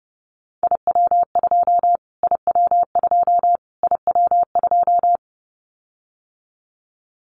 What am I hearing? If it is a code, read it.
Morse code, heard as a keyed tone, about 30 wpm, 700 Hz. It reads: IW2 IW2 IW2
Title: Callsign Prefixes for Top 25 DXCC - Repeated 3x - 30wpm